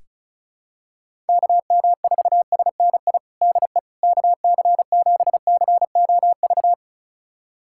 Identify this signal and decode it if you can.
Morse code, heard as a keyed tone, about 36 wpm, 700 Hz. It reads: KM4SNI DE KC7COV
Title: USA Callsign DE Callsign 35wpm